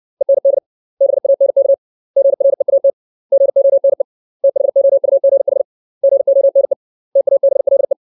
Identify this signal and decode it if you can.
Morse code, heard as a keyed tone, about 60 wpm, 550 Hz. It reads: END BANK GREAT GONE THOUGH GONE TABLE
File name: 632 words / 632 Words 60WPM